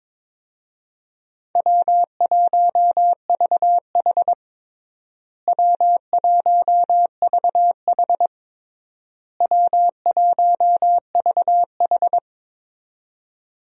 Morse code, heard as a keyed tone, about 22 wpm, 700 Hz. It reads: W1VH W1VH W1VH
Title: USA Call Signs - 4 Characters - Repeated 3x - 22wpm